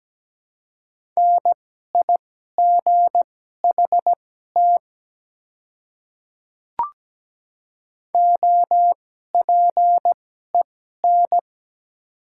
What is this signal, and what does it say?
Morse code, heard as a keyed tone, about 17 wpm, spaced as for 12 wpm, 700 Hz. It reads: NIGHT OPEN
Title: 2x Character Spacing - Top 300 Words - 17wpm